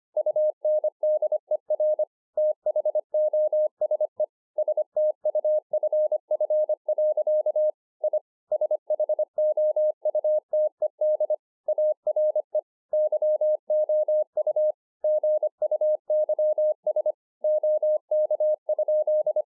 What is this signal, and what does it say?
Morse code, heard as a keyed tone, about 24 wpm, 610 Hz. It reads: UNDER THOSE STUFF. I SHOUTED ARE YOU GUYS OK?
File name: 2011-04-mp3 / THE 3.11 3 25WPM